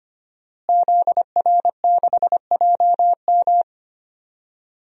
Morse code, heard as a keyed tone, about 25 wpm, 700 Hz. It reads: ZR6JM
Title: DXCC-100 Call Signs - 5 Characters - Repeated 2x - 25wpm